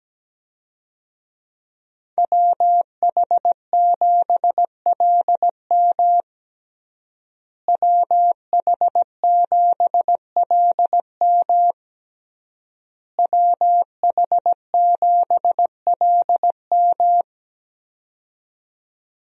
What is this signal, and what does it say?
Morse code, heard as a keyed tone, about 17 wpm, 700 Hz. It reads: WH7LM WH7LM WH7LM
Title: USA Call Signs - 5 Characters - Repeated 3x - 17wpm